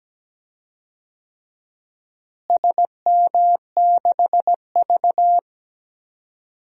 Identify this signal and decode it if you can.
Morse code, heard as a keyed tone, about 17 wpm, 700 Hz. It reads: SM6V
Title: DXCC-100 Call Signs - 4 Characters - Repeated 3x - 17wpm